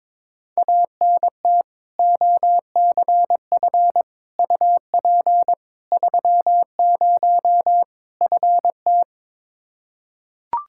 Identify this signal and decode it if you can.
Morse code, heard as a keyed tone, about 22 wpm, 700 Hz. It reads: ANT OCF UP 30 FT
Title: Practice QSO - Exchange 2 - Info - 22wpm